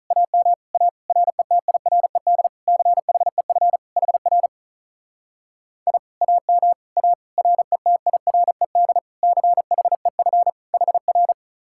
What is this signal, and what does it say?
Morse code, heard as a keyed tone, about 43 wpm, 700 Hz. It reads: AM A RETIRED CHEF HR I AM A RETIRED CHEF HR
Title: Speed-Racing Common QSO Phrases - 35wpm